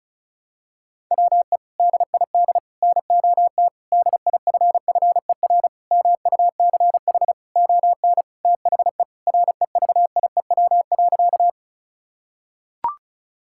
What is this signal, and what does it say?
Morse code, heard as a keyed tone, about 34 wpm, 690 Hz. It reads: WE DID NOT DIFFER MUCH ON THE REVIEW.
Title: Sentences from Top 1000 Words - Review Part 6 - 35wpm